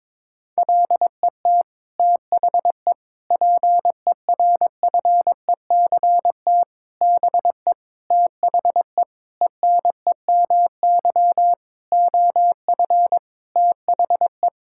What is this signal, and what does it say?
Morse code, heard as a keyed tone, about 22 wpm, 700 Hz. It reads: LET THE PERFECT BE THE ENEMY OF THE
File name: Sets of 9 Words - 22wpm